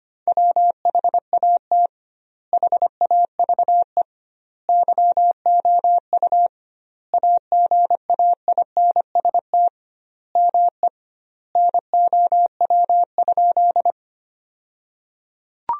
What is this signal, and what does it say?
Morse code, heard as a keyed tone, about 25 wpm, 700 Hz. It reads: WHAT HAVE YOU AGAINST ME NOW?
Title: 2x Word-Spacing - Sentences from Top 400 Words - 25wpm